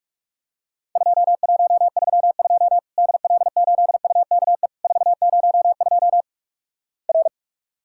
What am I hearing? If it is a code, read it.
Morse code, heard as a keyed tone, about 46 wpm, 700 Hz. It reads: 2132 DL8UKE 402 R
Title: Worked All Europe DX Contest - QTC Exchange - 45wpm